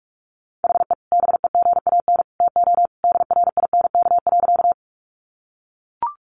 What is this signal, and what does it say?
Morse code, heard as a keyed tone, about 44 wpm, 700 Hz. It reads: HE BEGAN TO DRINK.
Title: Sentences from Top 1500 Words 45wpm